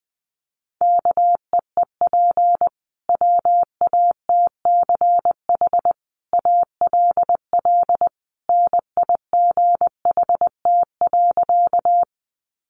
Morse code, heard as a keyed tone, about 20 wpm, 700 Hz. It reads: KEEP WATCH ALL NIGHT.
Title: Sentences from Top 300 Words 20wpm